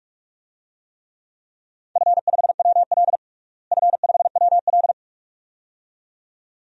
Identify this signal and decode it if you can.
Morse code, heard as a keyed tone, about 45 wpm, 700 Hz. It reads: F5WL F5WL
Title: DXCC-100 Call Signs - 4 Characters - Repeated 2x - 45wpm